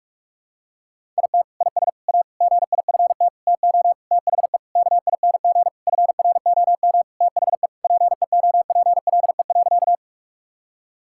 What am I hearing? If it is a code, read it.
Morse code, heard as a keyed tone, about 46 wpm, 700 Hz. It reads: IT IS A GIFT TO THE KING FROM THE PEOPLE.
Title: Sentences from Top 1000 Words - Review Part 6 - 45wpm